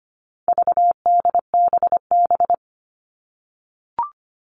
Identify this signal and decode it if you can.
Morse code, heard as a keyed tone, about 25 wpm, 700 Hz. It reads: VDBB
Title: All Cut Numbers - 4-Digit 25wpm